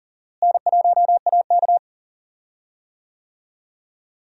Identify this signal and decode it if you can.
Morse code, heard as a keyed tone, about 40 wpm, 700 Hz. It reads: N1AK
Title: USA Call Signs - 4 Characters - Repeated 3x - 40wpm